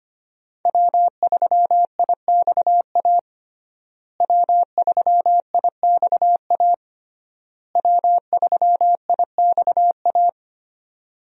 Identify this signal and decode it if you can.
Morse code, heard as a keyed tone, about 25 wpm, 700 Hz. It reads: W3IXA W3IXA W3IXA
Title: USA Call Signs - 5 Characters - Repeated 3x - 25wpm